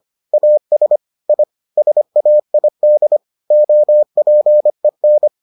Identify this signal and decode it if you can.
Morse code, heard as a keyed tone, about 25 wpm, 600 Hz. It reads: AS I SAID OPEN